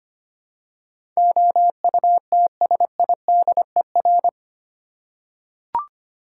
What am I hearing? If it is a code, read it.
Morse code, heard as a keyed tone, about 25 wpm, 700 Hz. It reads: OUTSIDER
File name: 8 Letter Words - First 3 Letters Given - 25wpm